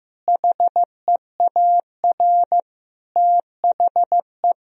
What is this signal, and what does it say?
Morse code, heard as a keyed tone, about 15 wpm, 700 Hz. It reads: HEAR THE
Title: Sentences from Top 1000 Words - Review Part 2 - 15wpm